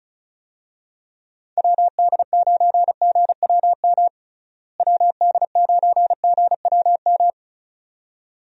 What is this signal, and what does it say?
Morse code, heard as a keyed tone, about 36 wpm, 700 Hz. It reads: WD9GWM WD9GWM
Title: USA Call Signs - 6 Characters - Repeated 2x - 35wpm